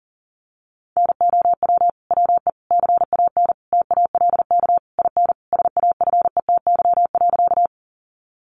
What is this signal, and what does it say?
Morse code, heard as a keyed tone, about 40 wpm, 700 Hz. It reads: NOW WE CAN TALK IN SAFETY.
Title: Sentences from Top 700 Words 40wpm